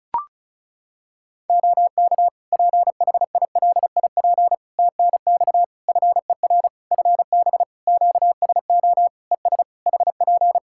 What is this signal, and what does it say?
Morse code, heard as a keyed tone, about 36 wpm, 700 Hz. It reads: OK PHILIP TNX FER FB QSO ES HP
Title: Practice QSO - Exchange 4 - Ending - 35wpm